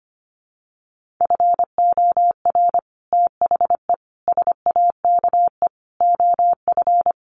Morse code, heard as a keyed tone, about 25 wpm, 700 Hz. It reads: FOR THE SAKE OF
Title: Sets of 5 Words Encore 25wpm